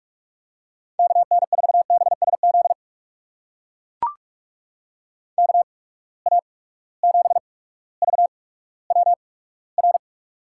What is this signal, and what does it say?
Morse code, heard as a keyed tone, about 43 wpm, spaced as for 19 wpm, 700 Hz. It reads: KN4BSZ KA7VWR
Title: USA Call Signs - 6 Characters - 8x Character Spacing - 45wpm